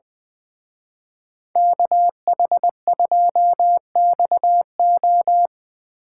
Morse code, heard as a keyed tone, about 23 wpm, 700 Hz. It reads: KH2XO
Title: Speed Racing - USA 2x2 Call Signs 20wpm